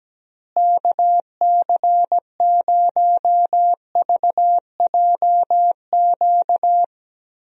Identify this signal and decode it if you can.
Morse code, heard as a keyed tone, about 17 wpm, 700 Hz. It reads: KC0VJQ
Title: USA Call Signs - 6 Characters - Repeated 3x - 17wpm